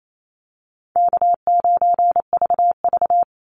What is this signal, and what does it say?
Morse code, heard as a keyed tone, about 28 wpm, 700 Hz. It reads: K9VV
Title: Call Signs Repeated 3x - 28wpm